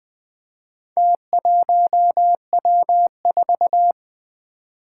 Morse code, heard as a keyed tone, about 20 wpm, 700 Hz. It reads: T1W4
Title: Four Challenging Letter-Number Combinations 20wpm